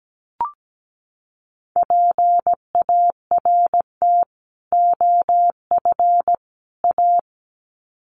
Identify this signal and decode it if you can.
Morse code, heard as a keyed tone, about 17 wpm, 700 Hz. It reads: PART OF A
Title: Sets of 3 Words 17wpm